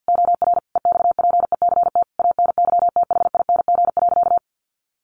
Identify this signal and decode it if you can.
Morse code, heard as a keyed tone, about 48 wpm, 700 Hz. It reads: OR EXPECT ANYTHING.
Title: Sentences from Top 1400 Words 50wpm